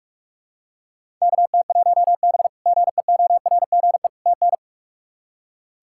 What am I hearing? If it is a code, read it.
Morse code, heard as a keyed tone, about 45 wpm, 700 Hz. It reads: KT1B GEORGE TN
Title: K1USN SST QSO Practice - 45wpm